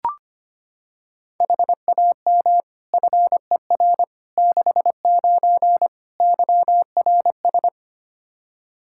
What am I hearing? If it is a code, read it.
Morse code, heard as a keyed tone, about 25 wpm, 700 Hz. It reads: HAM FER 69 YRS
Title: K1USN SST QSO Practice 25wpm